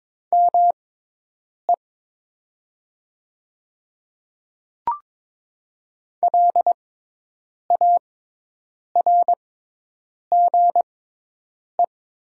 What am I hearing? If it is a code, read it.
Morse code, heard as a keyed tone, about 22 wpm, spaced as for 8 wpm, 700 Hz. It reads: ME LARGE
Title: 6x Character Spacing - Top 500 Words - 22wpm